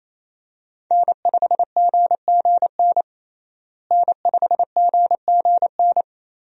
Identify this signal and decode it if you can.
Morse code, heard as a keyed tone, about 28 wpm, 700 Hz. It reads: N5GGN N5GGN
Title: USA Call Signs - 5 Characters - Repeated 3x - 28wpm